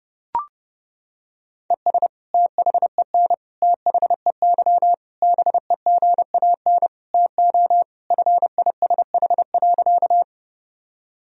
Morse code, heard as a keyed tone, about 32 wpm, 700 Hz. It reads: ES THEN THEY BEGAN TO FISH.
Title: Sentences from Top 300 Words and Abbreviations 30wpm